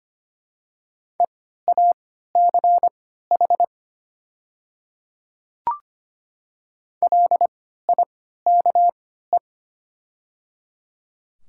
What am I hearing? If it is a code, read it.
Morse code, heard as a keyed tone, about 25 wpm, spaced as for 14 wpm, 700 Hz. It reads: EACH LIKE
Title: 3x Character Spacing - Top 200 Words - 25wpm